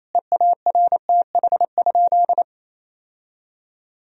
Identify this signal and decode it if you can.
Morse code, heard as a keyed tone, about 28 wpm, 700 Hz. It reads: EARTH?
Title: Sentences from Top 1000 Words - Review Part 2 - 28wpm